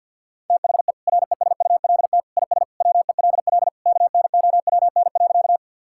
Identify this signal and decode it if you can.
Morse code, heard as a keyed tone, about 50 wpm, 700 Hz. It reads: THE RESULT IS WELL KNOWN.
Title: Sentences from Top 1000 Words - Review Part 5 - 50wpm